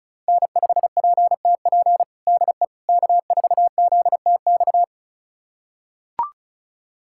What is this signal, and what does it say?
Morse code, heard as a keyed tone, about 36 wpm, 700 Hz. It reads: N5PTP DE K4ZTX
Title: USA Callsign DE Callsign - 5 Characters - 35wpm